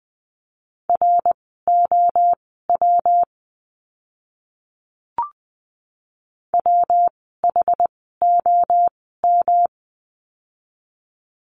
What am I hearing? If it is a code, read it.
Morse code, heard as a keyed tone, about 20 wpm, spaced as for 14 wpm, 700 Hz. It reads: ROW WHOM
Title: 115 Lesson 19f - Letter w - Words 20wpm